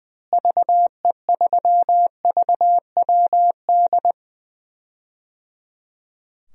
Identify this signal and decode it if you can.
Morse code, heard as a keyed tone, about 21 wpm, 700 Hz. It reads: VE3VWD
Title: DXCC-100 Call Signs - 6 Characters - Repeated 3x - 20wpm